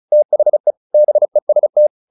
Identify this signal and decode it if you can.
Morse code, heard as a keyed tone, about 34 wpm, 600 Hz. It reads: THE BEST
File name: the old man and the sea / the old man and the sea 1 35wpm